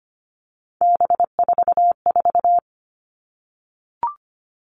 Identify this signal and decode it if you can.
Morse code, heard as a keyed tone, about 25 wpm, 700 Hz. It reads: B44
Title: All Cut Numbers - 3-Digit 25wpm